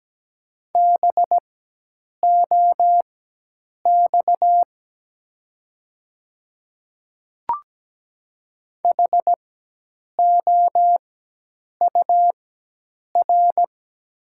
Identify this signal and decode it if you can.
Morse code, heard as a keyed tone, about 17 wpm, spaced as for 8 wpm, 700 Hz. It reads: BOX HOUR
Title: 4x Character Spacing - Top 500 Words - 17wpm